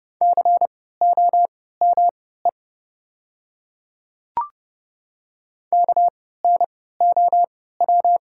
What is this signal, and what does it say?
Morse code, heard as a keyed tone, about 30 wpm, spaced as for 17 wpm, 690 Hz. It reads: COME KNOW
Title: 3x Character Spacing - Top 100 Words - 30wpm